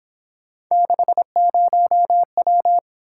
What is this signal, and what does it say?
Morse code, heard as a keyed tone, about 26 wpm, 700 Hz. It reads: 60W
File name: Speed-Racing - Three Letter-Number Combinations - 22wpm